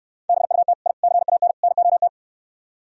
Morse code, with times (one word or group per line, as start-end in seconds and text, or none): 0.3-0.7 BUT
0.9-0.9 I
1.0-1.5 CAN
1.6-2.1 NOT